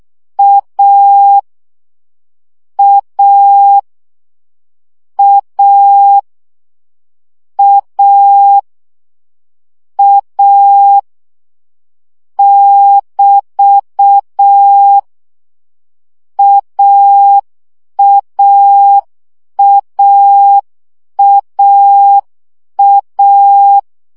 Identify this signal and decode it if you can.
Morse code, heard as a keyed tone, about 6 wpm, 800 Hz. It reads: A A A A A <BT> AAAAA
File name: CW gyakorlas a 5 baud